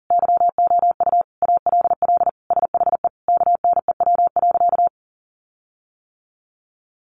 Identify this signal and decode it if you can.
Morse code, heard as a keyed tone, about 40 wpm, 700 Hz. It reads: YOU ALL SHE KNEW.
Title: Sentences from Top 1300 Words 40wpm